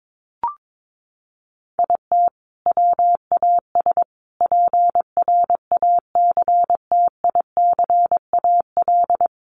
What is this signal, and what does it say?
Morse code, heard as a keyed tone, about 22 wpm, 700 Hz. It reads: IT WAS PRACTICAL
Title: Sentences from Top 1400 Words 22wpm